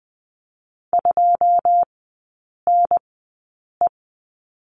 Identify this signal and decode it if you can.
Morse code, heard as a keyed tone, about 20 wpm, spaced as for 8 wpm, 690 Hz. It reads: 2NE
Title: 081 Lesson 14e - Number 2 - Three Characters 20wpm